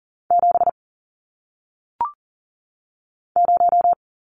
Morse code, heard as a keyed tone, about 41 wpm, 700 Hz. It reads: 7 0
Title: Numbers - 1-Digit 40wpm